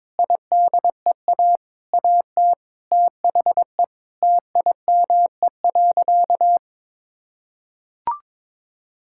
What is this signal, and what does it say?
Morse code, heard as a keyed tone, about 22 wpm, 700 Hz. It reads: IDEA AT THE TIME.
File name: practice / 22wpm / Sentences with 8-Word Phrases - 22wpm